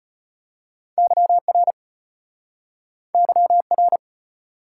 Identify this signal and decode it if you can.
Morse code, heard as a keyed tone, about 34 wpm, 700 Hz. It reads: YR YR